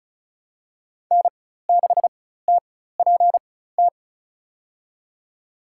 Morse code, heard as a keyed tone, about 34 wpm, spaced as for 16 wpm, 700 Hz. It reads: N6TPT
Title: USA Call Signs - 5 Characters - 4x Character Spacing - 35wpm